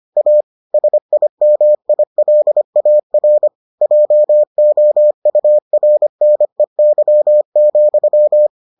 Morse code, heard as a keyed tone, about 25 wpm, 600 Hz. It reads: A SIMILAR JOURNEY,